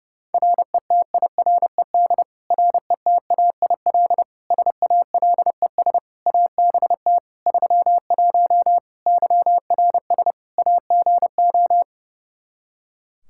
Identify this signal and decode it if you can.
Morse code, heard as a keyed tone, about 30 wpm, 700 Hz. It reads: RETIRED RETAIL SALES ABT 31 YRS AGO
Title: Practice QSO - Exchange 3 - More Info - 30wpm